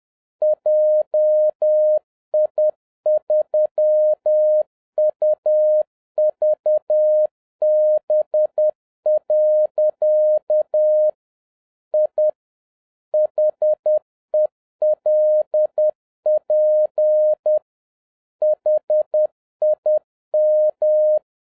Morse code, heard as a keyed tone, about 10 wpm, 610 Hz. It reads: JI3UVB. I HELP HIM